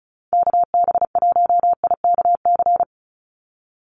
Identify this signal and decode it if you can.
Morse code, heard as a keyed tone, about 34 wpm, 700 Hz. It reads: KB1IKC
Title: USA Call Signs 35wpm